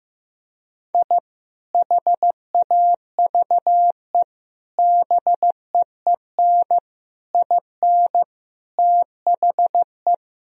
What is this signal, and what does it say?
Morse code, heard as a keyed tone, about 15 wpm, 700 Hz. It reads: I HAVE BEEN IN THE
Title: Sentences from Top 1000 Words - Review Part 3 - 15wpm